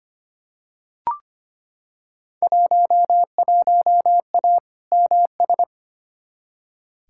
Morse code, heard as a keyed tone, about 25 wpm, 700 Hz. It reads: 11A MS
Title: Field Day Exchanges Encore - 25wpm